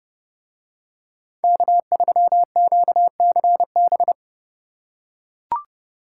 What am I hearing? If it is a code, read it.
Morse code, heard as a keyed tone, about 29 wpm, 700 Hz. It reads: K3QCB